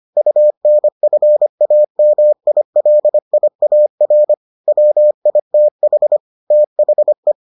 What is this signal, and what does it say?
Morse code, heard as a keyed tone, about 25 wpm, 600 Hz. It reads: UNFAMILIAR WITH THE